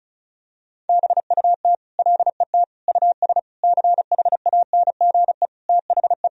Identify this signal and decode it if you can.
Morse code, heard as a keyed tone, about 34 wpm, 700 Hz. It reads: BUT LET US CHANGE THE
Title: Sentences from Top 1000 Words - Review Part 1 - 35wpm